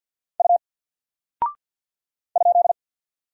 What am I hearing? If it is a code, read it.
Morse code, heard as a keyed tone, about 48 wpm, 700 Hz. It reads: U ?
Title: Single Character (Letter:Number:Punctuation:Prosign) 50wpm